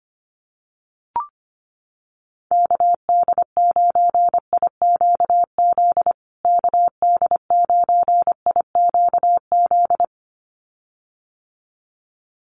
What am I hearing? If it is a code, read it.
Morse code, heard as a keyed tone, about 25 wpm, 700 Hz. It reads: KD9IQZ KD9IQZ
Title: USA Call Signs Repeated 2x 25wpm